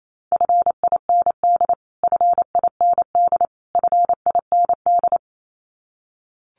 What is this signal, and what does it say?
Morse code, heard as a keyed tone, about 28 wpm, 700 Hz. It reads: FIND FIND FIND
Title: Top 500 Words Repeated 28wpm